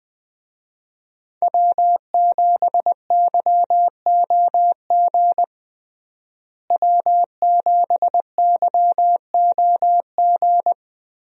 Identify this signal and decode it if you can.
Morse code, heard as a keyed tone, about 20 wpm, 700 Hz. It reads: W7YOG W7YOG
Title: USA Call Signs - 5 Characters - Repeated 3x - 20wpm